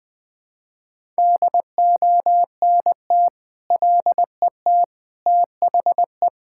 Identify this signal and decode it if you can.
Morse code, heard as a keyed tone, about 20 wpm, 700 Hz. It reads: DONT LET THE
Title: Sentences with 9-Word Phrases - 20wpm